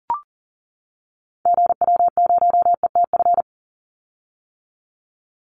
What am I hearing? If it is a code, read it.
Morse code, heard as a keyed tone, about 40 wpm, 700 Hz. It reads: GW0ETF
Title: Call Signs - 40wpm